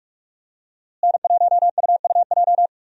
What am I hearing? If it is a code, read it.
Morse code, heard as a keyed tone, about 42 wpm, 700 Hz. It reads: N1UUJ
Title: USA Call Signs - 5 Characters - Repeated 3x - 45wpm